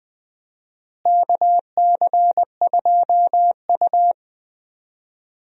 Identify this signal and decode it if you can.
Morse code, heard as a keyed tone, about 20 wpm, 700 Hz. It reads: KC2U
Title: USA Call Signs - 4 Characters - Repeated 2x - 20wpm